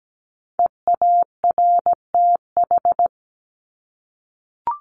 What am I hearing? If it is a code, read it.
Morse code, heard as a keyed tone, about 17 wpm, 700 Hz. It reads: EARTH
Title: Top 300 Words Repeated 17wpm